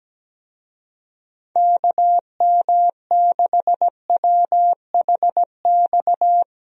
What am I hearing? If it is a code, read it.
Morse code, heard as a keyed tone, about 17 wpm, 700 Hz. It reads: KM6WHX
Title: USA Call Signs - 6 Characters - Repeated 3x - 17wpm